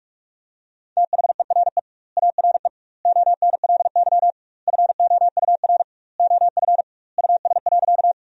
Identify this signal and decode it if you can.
Morse code, heard as a keyed tone, about 46 wpm, 700 Hz. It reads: THERE ARE ONLY FOUR OF US.
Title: 2x Word-Spacing - Sentences from Top 300 Words - 45wpm